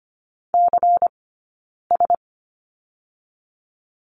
Morse code, heard as a keyed tone, about 25 wpm, spaced as for 9 wpm, 700 Hz. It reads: CS
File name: Lesson 17d - Letter u - Two Characters 25wpm